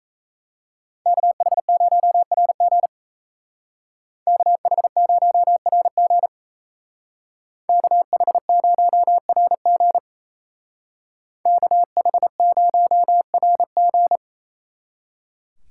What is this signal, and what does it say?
Morse code, heard as a keyed tone, about 35 wpm, 700 Hz. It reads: KH0RG KH0RG KH0RG KH0RG
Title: Speed Racing - USA 2x2 Call Signs 28wpm